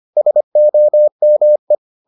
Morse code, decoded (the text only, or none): SOME